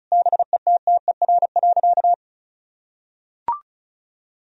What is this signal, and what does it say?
Morse code, heard as a keyed tone, about 36 wpm, 700 Hz. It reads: BETTER.
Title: Sentences from Top 1000 Words - Review Part 4 - 35wpm